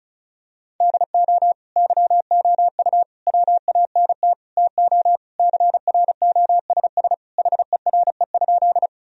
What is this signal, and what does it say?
Morse code, heard as a keyed tone, about 36 wpm, 700 Hz. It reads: DO YOU WANT TO CROSS HERE?
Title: Sentences from Top 1000 Words - Review Part 1 - 35wpm